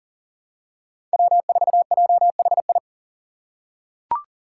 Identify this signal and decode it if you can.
Morse code, heard as a keyed tone, about 38 wpm, 700 Hz. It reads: W4JHI
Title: Five Challenging Letter-Number Combinations 40wpm